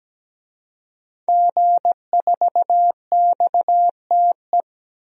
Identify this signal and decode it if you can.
Morse code, heard as a keyed tone, about 17 wpm, 700 Hz. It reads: G4XTE
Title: DXCC-100 Call Signs - 5 Characters - 17wpm